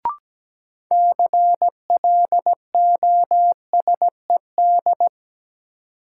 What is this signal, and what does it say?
Morse code, heard as a keyed tone, about 17 wpm, 700 Hz. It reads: CLOSED
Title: Sentences from Top 1400 Words - New Word Drill - 17wpm